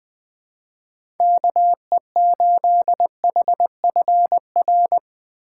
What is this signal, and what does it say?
Morse code, heard as a keyed tone, about 20 wpm, 700 Hz. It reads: KE8HFR
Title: USA Call Signs - 6 Characters - 2x Character Spacing - 20wpm